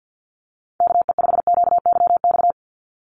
Morse code, heard as a keyed tone, about 50 wpm, 700 Hz. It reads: KE5QYX
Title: USA Call Signs 50wpm